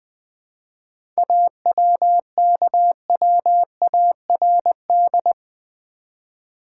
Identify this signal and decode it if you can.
Morse code, heard as a keyed tone, about 20 wpm, 690 Hz. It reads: AWKWARD
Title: 7 Letter Words - First 3 Letters Given - 20wpm